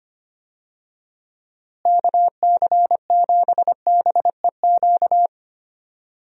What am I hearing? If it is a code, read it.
Morse code, heard as a keyed tone, about 25 wpm, 700 Hz. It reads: KC7BEQ
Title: USA Call Signs - 6 Characters - Repeated 2x - 25wpm